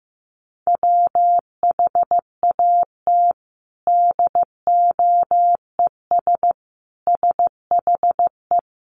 Morse code, heard as a keyed tone, about 15 wpm, 700 Hz. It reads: WHAT DOES SHE
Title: Sentences from Top 1500 Words 15wpm